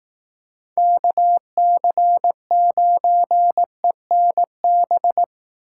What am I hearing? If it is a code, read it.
Morse code, heard as a keyed tone, about 18 wpm, 700 Hz. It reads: KC9ENB
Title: Speed Racing - USA 2x3 Call Signs 15wpm